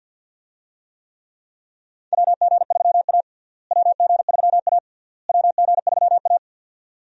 Morse code, heard as a keyed tone, about 50 wpm, 700 Hz. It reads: WG3A WG3A WG3A
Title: USA Call Signs - 4 Characters - Repeated 3x - 50wpm